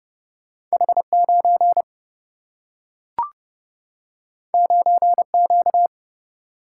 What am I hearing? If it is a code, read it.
Morse code, heard as a keyed tone, about 31 wpm, 700 Hz. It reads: H9 9Q
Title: Single-Letter Single-Number 30wpm